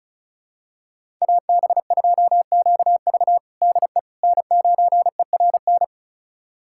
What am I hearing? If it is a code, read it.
Morse code, heard as a keyed tone, about 34 wpm, 690 Hz. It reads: AB2QV DE N9ERN
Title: USA Callsign DE Callsign - 5 Characters - 35wpm